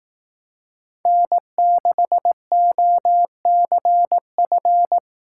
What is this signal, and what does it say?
Morse code, heard as a keyed tone, about 17 wpm, 700 Hz. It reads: N6OCF
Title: Speed Racing - USA 1x3 Call Signs 15wpm